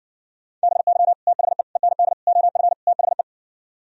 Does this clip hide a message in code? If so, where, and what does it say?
Yes: Morse code from 0.6 to 3.2 s, heard as a keyed tone, about 60 wpm, 700 Hz. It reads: BY THE END OF THE